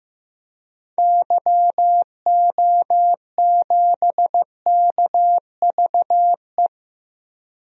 Morse code, heard as a keyed tone, about 15 wpm, 700 Hz. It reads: YO7KVE
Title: DXCC-100 Call Signs - 6 Characters - Repeated 3x - 15wpm